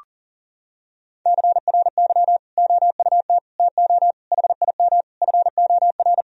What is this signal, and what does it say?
Morse code, heard as a keyed tone, about 40 wpm, 700 Hz. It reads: CRY OUT TO HIM FOR
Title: Sentences from Top 1000 Words - Review Part 2 - 40wpm